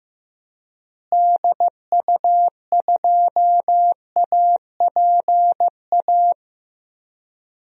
Morse code, heard as a keyed tone, about 15 wpm, 700 Hz. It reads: DU2APA
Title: DXCC-100 Call Signs - 6 Characters - Repeated 3x - 15wpm